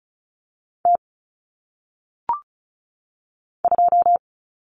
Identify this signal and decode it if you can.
Morse code, heard as a keyed tone, about 36 wpm, 700 Hz. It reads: T 2
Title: Common Cut Numbers - 1-Digit 35wpm